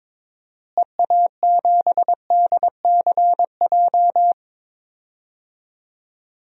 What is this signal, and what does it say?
Morse code, heard as a keyed tone, about 22 wpm, 700 Hz. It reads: EA7DCJ
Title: DXCC-100 Call Signs - 6 Characters - Repeated 2x - 22wpm